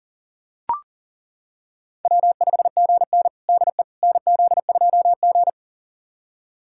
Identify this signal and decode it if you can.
Morse code, heard as a keyed tone, about 40 wpm, 700 Hz. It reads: W5GN DE NZ2G
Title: USA Callsign DE Callsign - 4 Characters - 40wpm